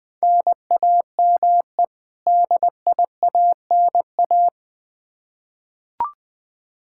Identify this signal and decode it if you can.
Morse code, heard as a keyed tone, about 20 wpm, 700 Hz. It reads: NAME DIANA
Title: Practice QSO - Exchange 1 - Beginning - 20wpm